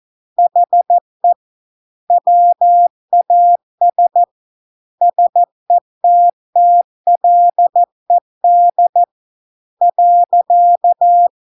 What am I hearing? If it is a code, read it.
Morse code, heard as a keyed tone, about 14 wpm, 700 Hz. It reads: HE WAS SETTLED .